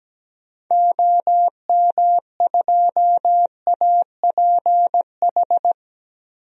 Morse code, heard as a keyed tone, about 17 wpm, 700 Hz. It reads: OM2APH
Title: DXCC-100 Call Signs - 6 Characters - 17wpm